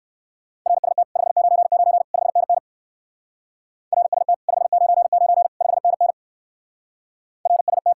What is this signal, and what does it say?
Morse code, heard as a keyed tone, about 56 wpm, 700 Hz. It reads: RST 599 5NN RST 599 5NN RST
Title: Speed-Racing Common QSO Phrases - 45wpm